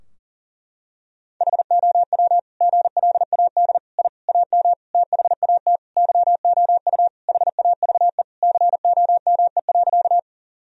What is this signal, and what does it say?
Morse code, heard as a keyed tone, about 40 wpm, 700 Hz. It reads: HOW GLAD I AM THAT YOU HAVE COME.
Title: Sentences from Top 1000 Words - Review Part 2 - 40wpm